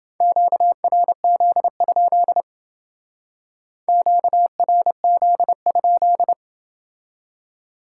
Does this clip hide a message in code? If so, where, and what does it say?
Yes: Morse code from 0.2 to 6.3 s, heard as a keyed tone, about 29 wpm, 700 Hz. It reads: QRZ? QRZ?